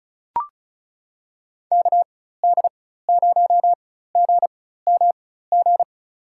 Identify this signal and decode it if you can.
Morse code, heard as a keyed tone, about 36 wpm, spaced as for 16 wpm, 700 Hz. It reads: KD0GMG
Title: USA Call Signs - 6 Characters - 4x Character Spacing - 35wpm